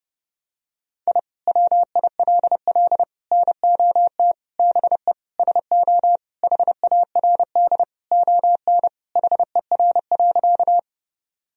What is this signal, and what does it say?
Morse code, heard as a keyed tone, about 32 wpm, 700 Hz. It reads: I WILL NOT BE SO HARD ON HER.
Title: Sentences from Top 1000 Words - Review Part 1 - 30wpm